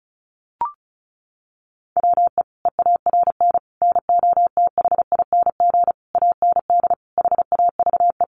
Morse code, heard as a keyed tone, about 34 wpm, 700 Hz. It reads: WE EARN NOTHING AND HAVE
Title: Sentences from Top 1500 Words 35wpm